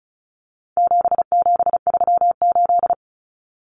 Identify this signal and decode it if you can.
Morse code, heard as a keyed tone, about 34 wpm, 700 Hz. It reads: 7738
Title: Numbers - 4-Digit 35wpm